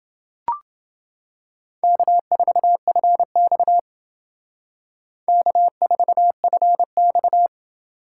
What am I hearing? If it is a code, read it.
Morse code, heard as a keyed tone, about 28 wpm, 700 Hz. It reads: K4FX K4FX